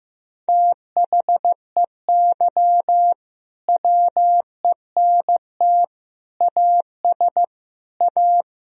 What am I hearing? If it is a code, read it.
Morse code, heard as a keyed tone, about 16 wpm, 700 Hz. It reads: THEY WENT AS A
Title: Sentences from Top 1000 Words - Review Part 1 - 15wpm